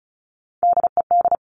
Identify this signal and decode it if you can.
Morse code, heard as a keyed tone, about 36 wpm, 700 Hz. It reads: DED
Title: Top Three Letter Combinations in Words 35wpm